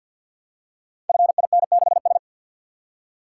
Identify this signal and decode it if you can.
Morse code, heard as a keyed tone, about 47 wpm, 700 Hz. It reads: FIN6S
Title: Code Groups - 5 Letter:Numbers - 50wpm